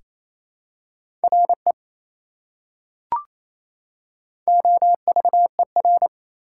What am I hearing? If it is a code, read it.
Morse code, heard as a keyed tone, about 28 wpm, 700 Hz. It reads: RE OVER
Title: Prefixes and Suffixes 28wpm